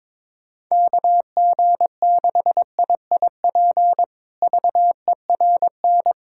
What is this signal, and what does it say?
Morse code, heard as a keyed tone, about 21 wpm, 700 Hz. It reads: KG6IIP VERN
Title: K1USN SST QSO Practice - 22wpm